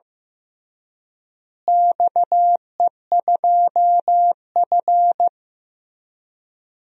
Morse code, heard as a keyed tone, about 15 wpm, 700 Hz. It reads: XE2F
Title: DXCC-100 Call Signs - 4 Characters - Repeated 3x - 15wpm